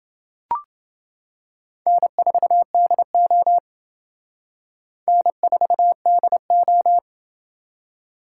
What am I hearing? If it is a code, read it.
Morse code, heard as a keyed tone, about 27 wpm, 700 Hz. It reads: N4DO N4DO